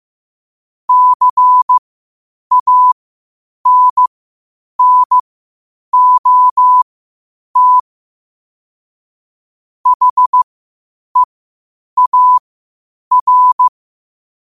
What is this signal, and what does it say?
Morse code, heard as a keyed tone, about 15 wpm, spaced as for 8 wpm, 1000 Hz. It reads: CANNOT HEAR
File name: qotdfiles / 8wpm